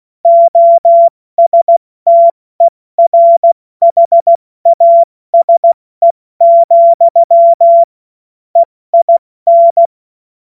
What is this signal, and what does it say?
Morse code, heard as a keyed tone, about 16 wpm, 680 Hz. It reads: OSTERHASE, EIN
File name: Witz-21-Oktober-2025-70-F80